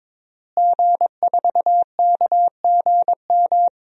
Morse code, heard as a keyed tone, about 22 wpm, 700 Hz. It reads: G4KGM
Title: DXCC-100 Call Signs - 5 Characters - Repeated 2x - 22wpm